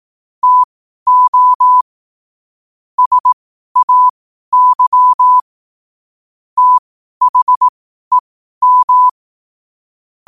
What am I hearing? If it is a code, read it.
Morse code, heard as a keyed tone, about 18 wpm, spaced as for 12 wpm, 1000 Hz. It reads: TO SAY THEM
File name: qotdfiles / 12wpm